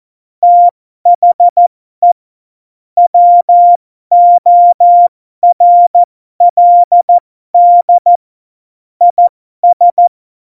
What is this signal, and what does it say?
Morse code, heard as a keyed tone, about 14 wpm, spaced as for 12 wpm, 700 Hz. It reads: THE WORLD IS
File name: Morsekurs / Witze